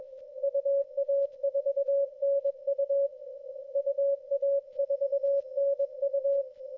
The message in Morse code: UA4NU UA4NU